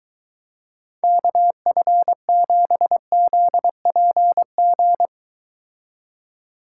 Morse code, heard as a keyed tone, about 23 wpm, 700 Hz. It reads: KF7ZPG